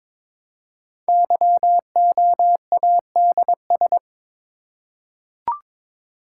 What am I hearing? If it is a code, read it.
Morse code, heard as a keyed tone, about 22 wpm, 690 Hz. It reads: YOADS
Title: Code Groups - 5 Letter:Numbers - 22wpm